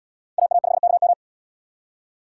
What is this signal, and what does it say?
Morse code, heard as a keyed tone, about 77 wpm, 690 Hz. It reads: AA5KM